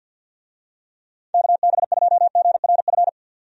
Morse code, heard as a keyed tone, about 50 wpm, 700 Hz. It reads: KB2GRF
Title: USA Call Signs - 6 Characters - 50wpm